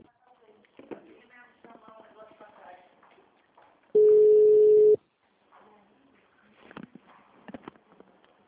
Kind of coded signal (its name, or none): telephone call progress tone